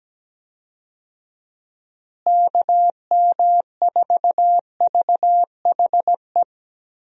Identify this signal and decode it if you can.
Morse code, heard as a keyed tone, about 17 wpm, 700 Hz. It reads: KM4VHE
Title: USA Call Signs - 6 Characters - Repeated 2x - 17wpm